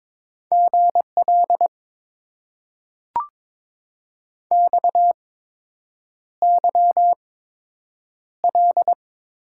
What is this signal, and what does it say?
Morse code, heard as a keyed tone, about 22 wpm, spaced as for 8 wpm, 700 Hz. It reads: GL XYL
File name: Top 90 QSO Elements - 8x Character Spacing - 22wpm